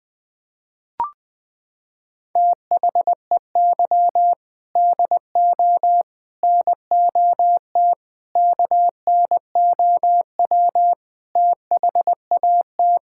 Morse code, heard as a keyed tone, about 21 wpm, 700 Hz. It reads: THEY DO NOT KNOW THAT
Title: Sentences with 4-Word Phrases - 20wpm